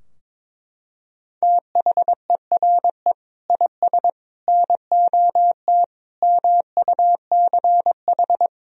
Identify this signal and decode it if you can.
Morse code, heard as a keyed tone, about 22 wpm, 700 Hz. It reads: THERE IS NOT MUCH
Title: Sentences from Top 1000 Words - Review Part 4 - 22wpm